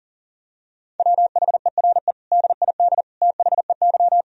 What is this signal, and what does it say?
Morse code, heard as a keyed tone, about 40 wpm, 700 Hz. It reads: WHERE DID THEY
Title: Sentences from Top 400 Words and Abbreviations 40wpm